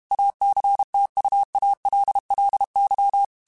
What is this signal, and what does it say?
Morse code, heard as a keyed tone, about 32 wpm, 800 Hz. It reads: ACTUALLY
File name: QSOs / QSOs-mp3s / WA7QOX 2@30